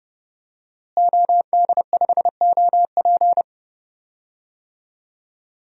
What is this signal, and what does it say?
Morse code, heard as a keyed tone, about 32 wpm, 700 Hz. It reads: OD5OP
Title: DXCC-100 Call Signs - 5 Characters - Repeated 2x - 30wpm